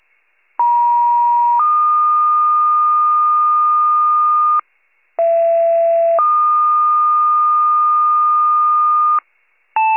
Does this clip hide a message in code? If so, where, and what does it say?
Yes: two-tone page from 0.6 to 9.2 s, heard as a one-second tone, then a three-second tone, about 950 then 1220 Hz, about 670 then 1220 Hz.